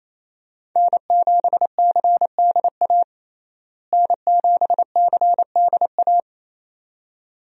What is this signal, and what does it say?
Morse code, heard as a keyed tone, about 27 wpm, 700 Hz. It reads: N7CDA N7CDA
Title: USA Call Signs - 5 Characters - Repeated 2x - 28wpm